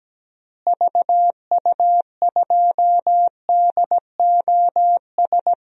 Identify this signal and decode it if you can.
Morse code, heard as a keyed tone, about 17 wpm, 700 Hz. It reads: VU2DOS
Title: DXCC-100 Call Signs - 6 Characters - Repeated 2x - 17wpm